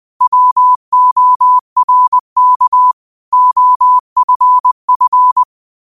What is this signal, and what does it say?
Morse code, heard as a keyed tone, about 20 wpm, 1000 Hz. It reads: WORK OFF
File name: qotdfiles / 20wpm